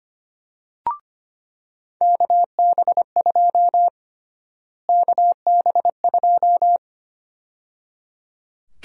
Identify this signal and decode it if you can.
Morse code, heard as a keyed tone, about 25 wpm, 700 Hz. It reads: KB2 KB2
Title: USA Call Sign Prefixes - Repeated 2x - 25wpm